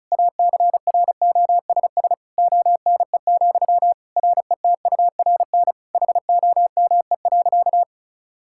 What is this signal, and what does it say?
Morse code, heard as a keyed tone, about 36 wpm, 700 Hz. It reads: ACROSS ONE, RETURN HOME.
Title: Sentences from Top 1000 Words - Review Part 5 - 35wpm